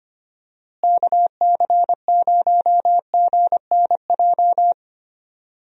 Morse code, heard as a keyed tone, about 25 wpm, 700 Hz. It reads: KC0GNJ
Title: USA Call Signs - 6 Characters - Repeated 3x - 25wpm